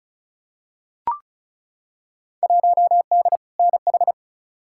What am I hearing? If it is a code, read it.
Morse code, heard as a keyed tone, about 36 wpm, 700 Hz. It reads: 1D NH
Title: Field Day Exchanges - 35wpm